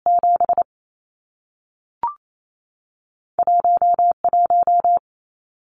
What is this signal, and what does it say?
Morse code, heard as a keyed tone, about 28 wpm, spaced as for 3 wpm, 700 Hz. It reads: 71 1
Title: Numbers - 2-Digit 28wpm